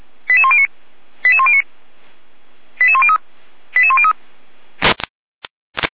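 Five-tone selective call (ZVEI), heard as quick runs of five tones, tones of about 70 ms. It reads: 88109 88109 88103 88103
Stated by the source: ZVEI five-tone selective call 88295 received via SDR, archived by Feuerwehr Orlamünde (fire brigade).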